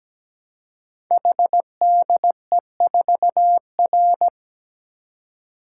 Morse code, heard as a keyed tone, about 17 wpm, 700 Hz. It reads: HDE4R
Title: Code Groups - 5 Letter:Numbers - 17wpm